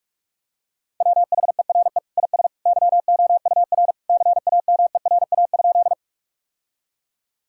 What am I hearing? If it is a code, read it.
Morse code, heard as a keyed tone, about 46 wpm, 700 Hz. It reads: WHERE IS YOUR CAMERA?
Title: Sentences from Top 1000 Words - Review Part 5 - 45wpm